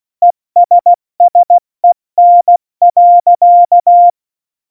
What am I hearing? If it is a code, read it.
Morse code, heard as a keyed tone, about 16 wpm, 700 Hz. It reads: ESSEN.
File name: Morsekurs / Witze / Witz 14. Januar 2020, 70 F80